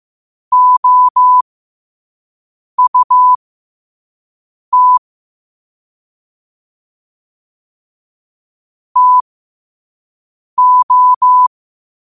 Morse code, heard as a keyed tone, about 15 wpm, spaced as for 5 wpm, 1000 Hz. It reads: OUT TO
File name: qotdfiles / 5wpm